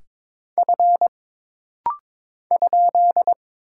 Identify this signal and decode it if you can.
Morse code, heard as a keyed tone, about 22 wpm, 700 Hz. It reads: F ?
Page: This rapid-fire timing is geared to push your ICR (Instant Character Recognition) skills.